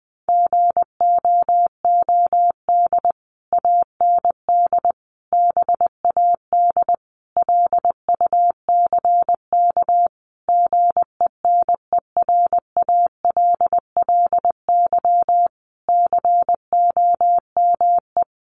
Morse code, heard as a keyed tone, about 20 wpm, 700 Hz. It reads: GOOD AND BAD LUCK GENERALLY COME
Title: Sentences from Top 1500 Words 20wpm